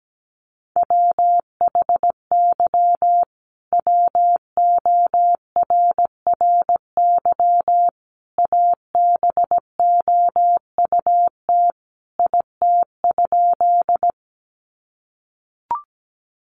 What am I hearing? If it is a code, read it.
Morse code, heard as a keyed tone, about 17 wpm, 700 Hz. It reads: WHY WORRY ABOUT IT?
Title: Sentences from Top 1100 Words 17wpm